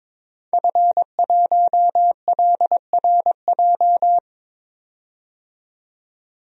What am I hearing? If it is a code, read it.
Morse code, heard as a keyed tone, about 22 wpm, 700 Hz. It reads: F1LRJ
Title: DXCC-100 Call Signs - 5 Characters - 22wpm